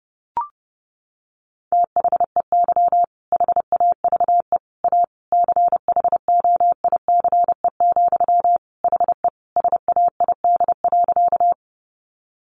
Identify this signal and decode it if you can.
Morse code, heard as a keyed tone, about 30 wpm, 700 Hz. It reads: THEY HAVE A CHOICE, HE SAID.
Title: Sentences from Top 900 Words 30wpm